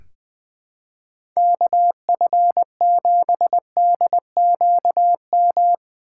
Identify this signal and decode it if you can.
Morse code, heard as a keyed tone, about 21 wpm, 700 Hz. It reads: KF7DQM
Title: USA Call Signs - 2x Character Spacing - 20wpm